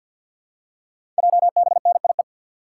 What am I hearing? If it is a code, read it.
Morse code, heard as a keyed tone, about 50 wpm, 690 Hz. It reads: JBNIE
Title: Code Groups - 5 Letters - 50wpm